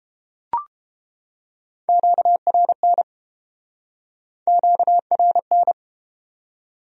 Morse code, heard as a keyed tone, about 33 wpm, 700 Hz. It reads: QRN QRN